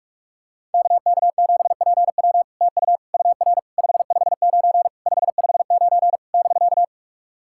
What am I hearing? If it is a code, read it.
Morse code, heard as a keyed tone, about 44 wpm, 700 Hz. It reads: KK7PW TU UR 559 559 <BK>
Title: SOTA QSO Practice 45wpm